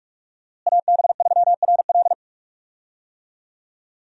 Morse code, heard as a keyed tone, about 46 wpm, 700 Hz. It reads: AB3RL
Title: USA Call Signs - 5 Characters - Repeated 3x - 45wpm